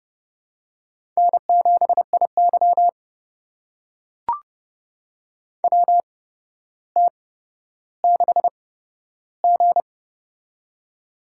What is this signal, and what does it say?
Morse code, heard as a keyed tone, about 30 wpm, spaced as for 13 wpm, 700 Hz. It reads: N7IY WT6G
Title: USA Call Signs - 8x Character Spacing - 30wpm